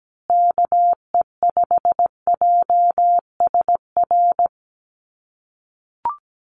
Morse code, heard as a keyed tone, about 17 wpm, 700 Hz. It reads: KE5JSR
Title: USA Call Signs Repeated 3x 17wpm